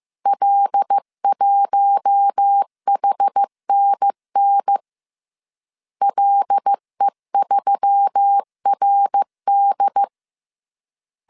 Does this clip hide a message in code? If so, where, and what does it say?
Yes: Morse code from 0.3 to 10.0 s, heard as a keyed tone, about 15 wpm, 800 Hz. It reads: L1HNN LE3RD